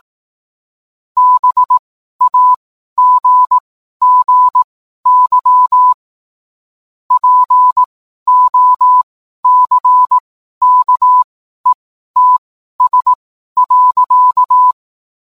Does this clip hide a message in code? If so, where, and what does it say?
Yes: Morse code from 1.2 to 14.7 s, heard as a keyed tone, about 18 wpm, spaced as for 12 wpm, 1000 Hz. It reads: BAGGY POCKETS.